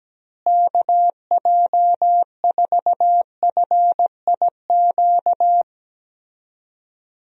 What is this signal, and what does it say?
Morse code, heard as a keyed tone, about 17 wpm, 700 Hz. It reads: KJ4FIQ
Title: USA Call Signs - 6 Characters - Repeated 3x - 17wpm